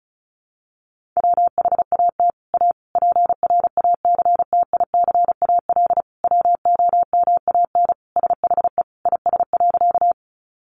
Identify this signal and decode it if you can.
Morse code, heard as a keyed tone, about 34 wpm, 700 Hz. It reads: WHAT A PRACTICAL WOMAN SHE IS.
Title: Sentences from Top 1400 Words 35wpm